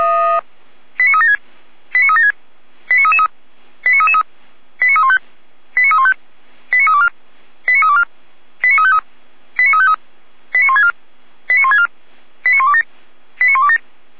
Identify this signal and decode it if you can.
Five-tone selective call (ZVEI), heard as quick runs of five tones, tones of about 70 ms. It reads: 89376 89376 89333 89333 89315 89315 89324 89324 89353 89353 89164 89164 89127 89127
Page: SDR/ZVEI